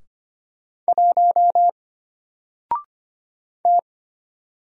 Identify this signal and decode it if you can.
Morse code, heard as a keyed tone, about 25 wpm, 700 Hz. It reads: 1 T
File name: Single-Letter-Number-25wpm